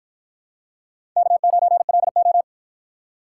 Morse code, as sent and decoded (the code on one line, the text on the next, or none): -.- ----. .-.. ---
K9LO